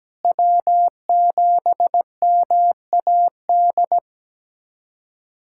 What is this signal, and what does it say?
Morse code, heard as a keyed tone, about 17 wpm, 700 Hz. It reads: W7MAD
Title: USA Callsign DE Callsign - 5 Characters - 17wpm